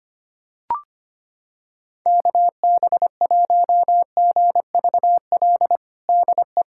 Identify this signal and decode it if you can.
Morse code, heard as a keyed tone, about 25 wpm, 700 Hz. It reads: KB1GVL DE
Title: USA Callsign DE Callsign - 6 Characters - 25wpm